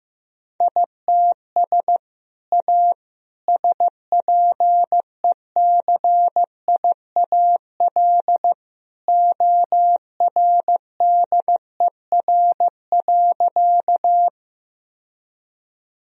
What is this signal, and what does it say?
Morse code, heard as a keyed tone, about 15 wpm, 700 Hz. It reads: ITS A SPECIAL ORDER.
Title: Sentences from Top 1000 Words - Review Part 3 - 15wpm